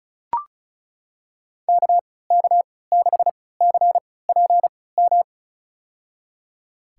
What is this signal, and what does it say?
Morse code, heard as a keyed tone, about 34 wpm, spaced as for 20 wpm, 700 Hz. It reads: KK6CPM
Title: USA Call Signs - 6 Characters - 3x Character Spacing - 35wpm